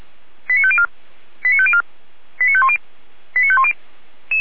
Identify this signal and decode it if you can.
Five-tone selective call (ZVEI), heard as quick runs of five tones, tones of about 70 ms. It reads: 89504 89504 89510 89510
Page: audio.f85360.feuerwehr-orlamuende.org